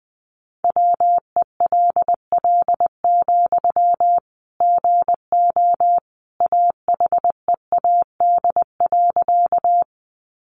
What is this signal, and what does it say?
Morse code, heard as a keyed tone, about 20 wpm, 700 Hz. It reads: WELL, GO AHEAD.
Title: Sentences from Top 1100 Words 20wpm